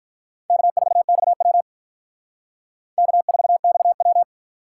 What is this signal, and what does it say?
Morse code, heard as a keyed tone, about 43 wpm, 700 Hz. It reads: K4XW K4XW
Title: Speed Racing - USA 1x2 Call Signs 35wpm